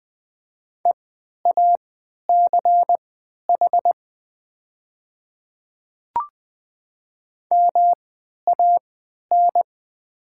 Morse code, heard as a keyed tone, about 20 wpm, spaced as for 11 wpm, 700 Hz. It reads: EACH MAN
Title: 3x Character Spacing - Top 200 Words - 20wpm